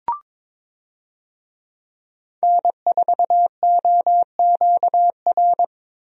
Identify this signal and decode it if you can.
Morse code, heard as a keyed tone, about 22 wpm, 700 Hz. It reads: N4OQR
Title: USA Call Signs - 5 Characters - Repeated 2x - 22wpm